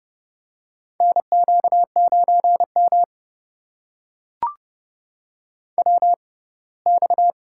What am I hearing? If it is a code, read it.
Morse code, heard as a keyed tone, about 30 wpm, spaced as for 20 wpm, 700 Hz. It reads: NQ9M WX
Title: USA Call Signs - 4 Characters - 6x Character Spacing - 30wpm